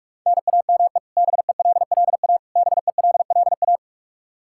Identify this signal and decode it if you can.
Morse code, heard as a keyed tone, about 42 wpm, 700 Hz. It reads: NAME BELLA BELLA
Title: Practice QSO - Exchange 1 - Beginning - 45wpm